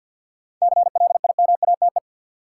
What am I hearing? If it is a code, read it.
Morse code, heard as a keyed tone, about 53 wpm, 700 Hz. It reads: CLIMATE
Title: Sentences from Top 1600 Words - New Word Drill - 50wpm